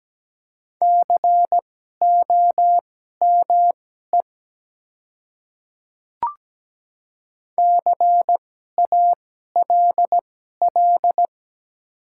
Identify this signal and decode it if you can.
Morse code, heard as a keyed tone, about 17 wpm, spaced as for 12 wpm, 700 Hz. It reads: COME CALL
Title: 2x Character Spacing - Top 100 Words - 17wpm